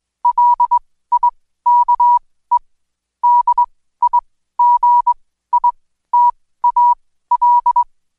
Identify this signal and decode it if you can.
Morse code, heard as a keyed tone, about 21 wpm, spaced as for 14 wpm, 970 Hz. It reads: LIKE DIGITAL
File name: Code Quick / AudioDownld2 / Seg123-Day28-DayDrill 28